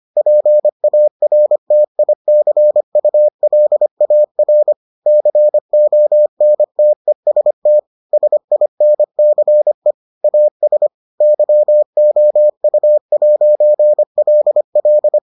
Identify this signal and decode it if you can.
Morse code, heard as a keyed tone, about 25 wpm, 600 Hz. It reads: PARTICULAR CONTEST SINCE AS YOU'LL